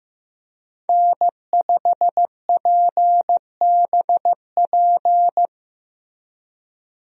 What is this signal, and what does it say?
Morse code, heard as a keyed tone, about 15 wpm, 700 Hz. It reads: N5PBP
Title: USA Call Signs - 5 Characters - Repeated 3x - 15wpm